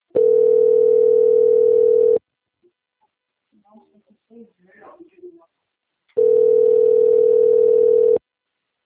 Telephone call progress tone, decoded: ringback tone

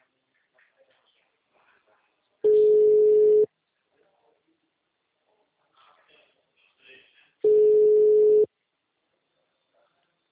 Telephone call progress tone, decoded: ringback tone

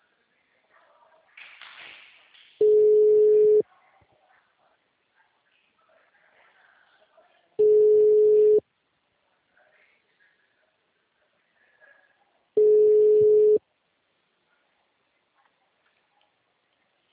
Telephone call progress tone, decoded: ringback tone

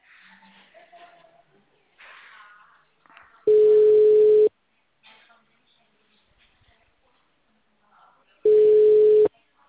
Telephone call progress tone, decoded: ringback tone

